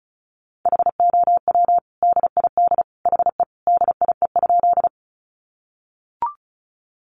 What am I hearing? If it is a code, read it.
Morse code, heard as a keyed tone, about 34 wpm, 700 Hz. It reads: HOW DID HE DIE?
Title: Sentences from Top 1300 Words 35wpm